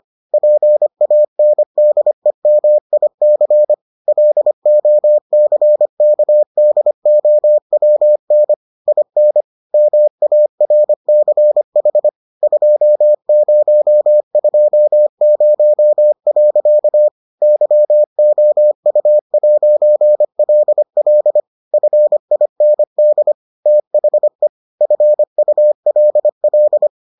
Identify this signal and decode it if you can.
Morse code, heard as a keyed tone, about 25 wpm, 600 Hz. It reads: PANDEMIC LOCKDOWN IN MARCH 2020. YOU'LL FIND THE FULL